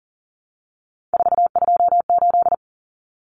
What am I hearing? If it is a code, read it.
Morse code, heard as a keyed tone, about 40 wpm, 700 Hz. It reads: 428
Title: Common Cut Numbers - 3-Digit 40wpm